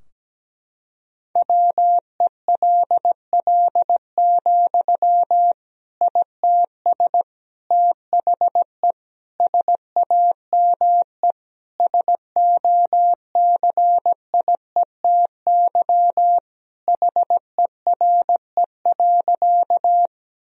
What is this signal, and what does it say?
Morse code, heard as a keyed tone, about 17 wpm, 700 Hz. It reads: WELL, ITS THE SAME SOCIETY HERE.
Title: Sentences from Top 1000 Words - Review Part 4 - 17wpm